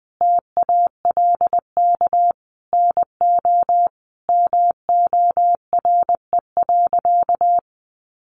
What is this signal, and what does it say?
Morse code, heard as a keyed tone, about 20 wpm, 700 Hz. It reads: TALK NO MORE.
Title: Sentences from Top 1400 Words 20wpm